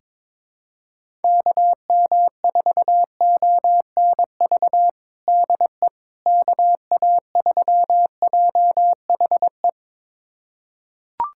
Morse code, heard as a keyed tone, about 22 wpm, 700 Hz. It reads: KM4ONV DE KA3JHE
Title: USA Callsign DE Callsign - 6 Characters - 22wpm